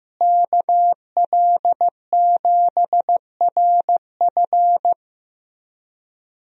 Morse code, heard as a keyed tone, about 15 wpm, 700 Hz. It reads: KL7RF
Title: DXCC-100 Call Signs - 5 Characters - Repeated 2x - 15wpm